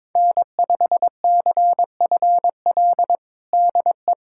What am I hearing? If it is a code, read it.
Morse code, heard as a keyed tone, about 22 wpm, 700 Hz. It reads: N5CFL DE
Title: USA Callsign DE Callsign - 5 Characters - 22wpm